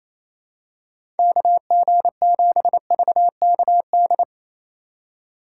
Morse code, heard as a keyed tone, about 27 wpm, 700 Hz. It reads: KG7VKD
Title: USA Call Signs - 2x Character Spacing - 28wpm